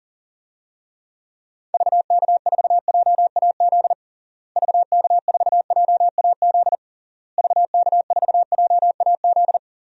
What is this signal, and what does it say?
Morse code, heard as a keyed tone, about 37 wpm, 700 Hz. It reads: VK4JAZ VK4JAZ VK4JAZ
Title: DXCC-100 Call Signs - 6 Characters - Repeated 3x - 40wpm